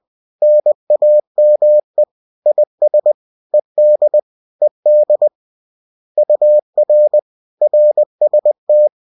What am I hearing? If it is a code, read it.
Morse code, heard as a keyed tone, about 20 wpm, 600 Hz. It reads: NAME IS ED ED UR RST